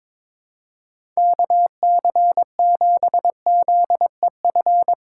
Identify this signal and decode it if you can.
Morse code, heard as a keyed tone, about 22 wpm, 700 Hz. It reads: KC7ZEF
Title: USA Call Signs - 6 Characters - 6x Character Spacing - 22wpm